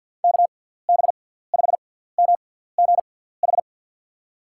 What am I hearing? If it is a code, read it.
Morse code, heard as a keyed tone, about 49 wpm, spaced as for 17 wpm, 700 Hz. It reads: KB5MGH
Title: USA Call Signs - 6 Characters - 6x Character Spacing - 50wpm